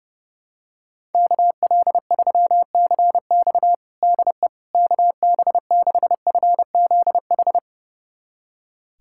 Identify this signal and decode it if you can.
Morse code, heard as a keyed tone, about 29 wpm, 700 Hz. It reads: KL3CX DE KB6FZH
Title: USA Callsign DE Callsign 30wpm